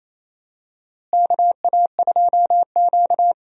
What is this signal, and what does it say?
Morse code, heard as a keyed tone, about 28 wpm, 700 Hz. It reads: KA2Q
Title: USA Call Signs - 4 Characters - Repeated 2x - 28wpm